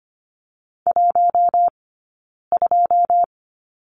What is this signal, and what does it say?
Morse code, heard as a keyed tone, about 25 wpm, 700 Hz. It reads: 1 2
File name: course / 25:10wpm / 099 Lesson 17d - Letter u - Two Characters 25wpm